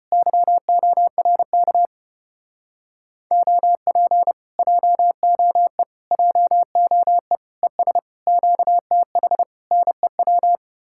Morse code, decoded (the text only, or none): YORK OP JOE JOE ES QTH NEW